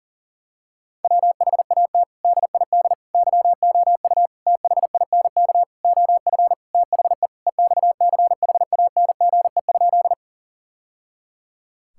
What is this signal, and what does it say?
Morse code, heard as a keyed tone, about 40 wpm, 700 Hz. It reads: WHAT DID YOU THINK OF THE EXCHANGE?
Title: Sentences from Top 1000 Words - Review Part 6 - 40wpm